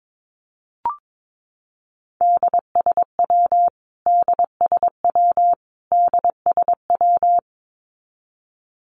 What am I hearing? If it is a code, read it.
Morse code, heard as a keyed tone, about 22 wpm, 700 Hz. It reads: DSW DSW DSW
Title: Abbreviations Repeated 3x - 22wpm